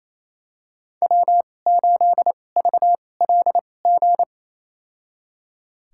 Morse code, heard as a keyed tone, about 28 wpm, spaced as for 20 wpm, 700 Hz. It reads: W8VLG
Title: USA Call Signs - 5 Characters - 2x Character Spacing - 28wpm